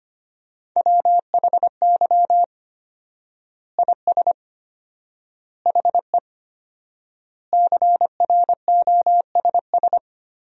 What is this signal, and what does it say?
Morse code, heard as a keyed tone, about 25 wpm, 700 Hz. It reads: WHY IS HE CROSS
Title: 4x Word-Spacing - Sentences from Top 300 Words - 25wpm